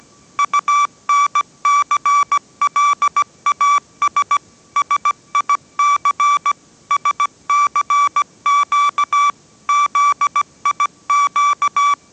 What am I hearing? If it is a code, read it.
Morse code, heard as a keyed tone, about 17 wpm, 1160 Hz. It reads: UNCLAS SIC SCQ ZIQ
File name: 18wpm